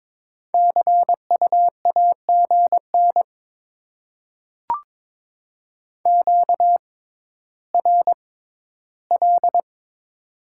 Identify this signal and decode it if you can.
Morse code, heard as a keyed tone, about 22 wpm, spaced as for 13 wpm, 700 Hz. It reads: CU AGN QRL
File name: Top 20 QSO Elements - 6x Character Spacing - 22wpm